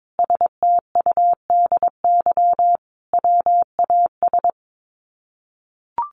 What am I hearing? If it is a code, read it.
Morse code, heard as a keyed tone, about 22 wpm, 700 Hz. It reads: STUDY WAS
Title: Sets of 5 Words 22wpm